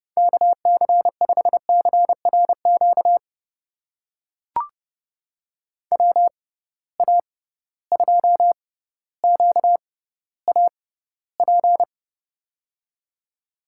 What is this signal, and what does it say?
Morse code, heard as a keyed tone, about 31 wpm, spaced as for 16 wpm, 700 Hz. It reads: KC5CRQ WA2QAP
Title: USA Call Signs - 6 Characters - 6x Character Spacing - 30wpm